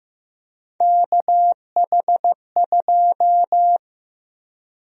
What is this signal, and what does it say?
Morse code, heard as a keyed tone, about 15 wpm, 700 Hz. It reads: KH2
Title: USA Call Sign Prefixes - 15wpm